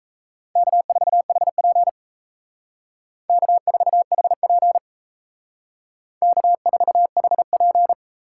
Morse code, heard as a keyed tone, about 36 wpm, 700 Hz. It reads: K4HP K4HP K4HP